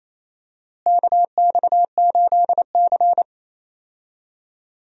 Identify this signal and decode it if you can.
Morse code, heard as a keyed tone, about 28 wpm, 700 Hz. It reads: KX8C
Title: USA Call Signs - 4 Characters - Repeated 2x - 28wpm